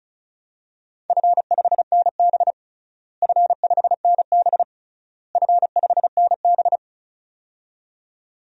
Morse code, heard as a keyed tone, about 36 wpm, 700 Hz. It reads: F5NB F5NB F5NB
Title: DXCC-100 Call Signs - 4 Characters - Repeated 3x - 35wpm